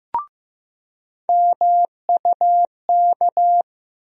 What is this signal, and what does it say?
Morse code, heard as a keyed tone, about 15 wpm, 700 Hz. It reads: MUK
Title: USA Call Sign Postfixes - 3 Letters 15wpm